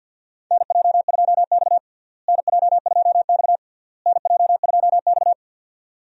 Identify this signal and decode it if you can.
Morse code, heard as a keyed tone, about 49 wpm, 700 Hz. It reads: NJ2X NJ2X NJ2X
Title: USA Call Signs - 4 Characters - Repeated 3x - 50wpm